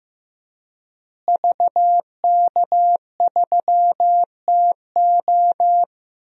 Morse code, heard as a keyed tone, about 15 wpm, 700 Hz. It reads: VK3TO
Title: DXCC-100 Call Signs - 5 Characters - Repeated 3x - 15wpm